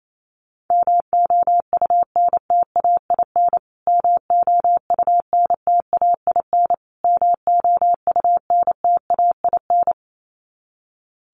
Morse code, heard as a keyed tone, about 28 wpm, 700 Hz. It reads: MOUNTAIN MOUNTAIN MOUNTAIN
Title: Top 300 Words Repeated 28wpm